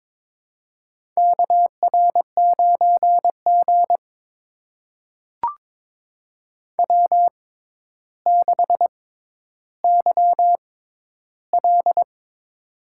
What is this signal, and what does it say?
Morse code, heard as a keyed tone, about 22 wpm, spaced as for 11 wpm, 700 Hz. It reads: KR9G W6YL
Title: USA Call Signs - 4 Characters - 6x Character Spacing - 22wpm